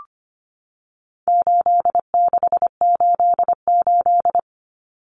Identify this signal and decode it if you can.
Morse code, heard as a keyed tone, about 25 wpm, 700 Hz. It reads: 8688
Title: Numbers - 4-Digit 25wpm